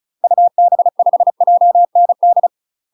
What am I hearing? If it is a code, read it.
Morse code, heard as a keyed tone, about 34 wpm, 700 Hz. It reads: UB5JND
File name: UK MASTER QSOs TRAINING/35WPM